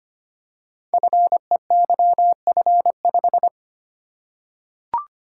Morse code, heard as a keyed tone, about 25 wpm, 700 Hz. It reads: FEYF5
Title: Code Groups - 5 Letter:Numbers - 25wpm